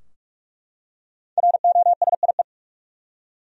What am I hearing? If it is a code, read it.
Morse code, heard as a keyed tone, about 46 wpm, 700 Hz. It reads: ROSIE
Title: USA Common Names - 45wpm